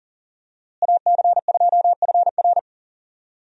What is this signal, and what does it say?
Morse code, heard as a keyed tone, about 38 wpm, 700 Hz. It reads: AC2FR
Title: USA Call Signs - 5 Characters - Repeated 3x - 40wpm